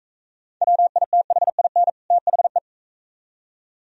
Morse code, heard as a keyed tone, about 44 wpm, 700 Hz. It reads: WITHIN THE